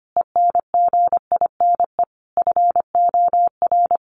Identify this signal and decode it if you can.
Morse code, heard as a keyed tone, about 25 wpm, 700 Hz. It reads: ENGINE FOR
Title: Sentences from Top 1500 Words 25wpm